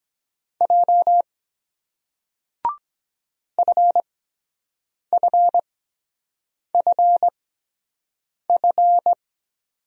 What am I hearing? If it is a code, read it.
Morse code, heard as a keyed tone, about 23 wpm, spaced as for 7 wpm, 700 Hz. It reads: J FFFF